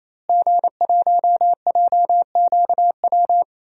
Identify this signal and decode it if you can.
Morse code, heard as a keyed tone, about 28 wpm, 700 Hz. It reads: G1JQW
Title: DXCC-100 Call Signs - 5 Characters - Repeated 3x - 28wpm